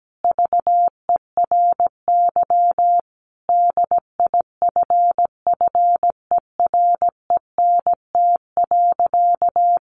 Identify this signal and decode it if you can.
Morse code, heard as a keyed tone, about 17 wpm, 700 Hz. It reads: VERY DIFFERENT.
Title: Sentences from Top 1300 Words 17wpm